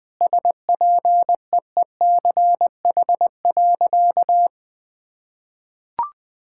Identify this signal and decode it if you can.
Morse code, heard as a keyed tone, about 20 wpm, 700 Hz. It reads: SPEECH.
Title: Sentences with 5-Word Phrases - Spoken Beginning - 20wpm